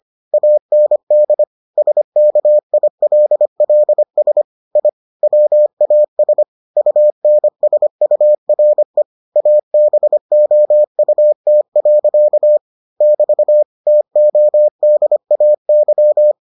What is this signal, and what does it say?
Morse code, heard as a keyed tone, about 25 wpm, 600 Hz. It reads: AND SKILLS I WAS UNSURE ABOUT. <BT> TODAY